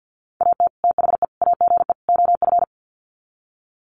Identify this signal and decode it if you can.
Morse code, heard as a keyed tone, about 50 wpm, 700 Hz. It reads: AT THE AGE OF
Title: Sets of 4 Words 50wpm